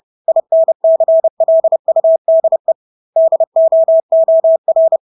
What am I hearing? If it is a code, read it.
Morse code, heard as a keyed tone, about 30 wpm, 650 Hz. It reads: INCLUDE DOOR
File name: Engelske ord 30-30 wpm